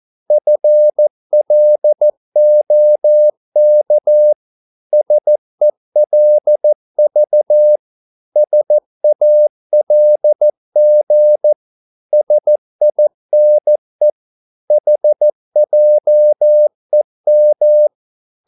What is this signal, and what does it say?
Morse code, heard as a keyed tone, about 14 wpm, 600 Hz. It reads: FLOK SELV SALG SINE HJEM
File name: Korte ord DK 14wpm